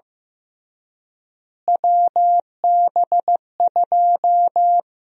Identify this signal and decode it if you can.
Morse code, heard as a keyed tone, about 15 wpm, 700 Hz. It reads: WB2
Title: USA Call Sign Prefixes - Repeated 2x - 15wpm